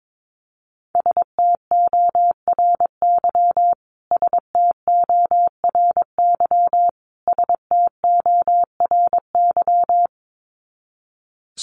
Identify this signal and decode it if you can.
Morse code, heard as a keyed tone, about 22 wpm, 700 Hz. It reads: STORY STORY STORY
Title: Top 500 Words Repeated 22wpm